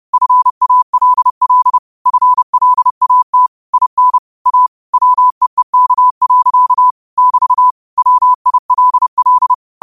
Morse code, heard as a keyed tone, about 30 wpm, 1000 Hz. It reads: FALL FLAT IN A WEEK. <BT> WILL
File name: qotdfiles / 30wpm